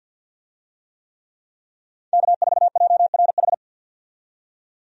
Morse code, heard as a keyed tone, about 51 wpm, 700 Hz. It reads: K4JRH
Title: USA Call Signs - 5 Characters - 50wpm